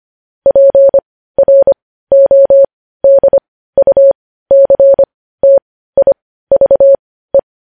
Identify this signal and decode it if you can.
Morse code, heard as a keyed tone, about 25 wpm, spaced as for 15 wpm, 550 Hz. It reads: PRODUCTIVE
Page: news headlines in morse code at 15 wpm